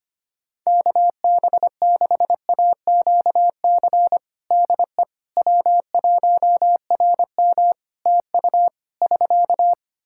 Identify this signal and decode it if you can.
Morse code, heard as a keyed tone, about 25 wpm, 700 Hz. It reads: KB6AQC DE W1RM TU <SK>
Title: Practice QSO - Exchange 4 - Ending - 25wpm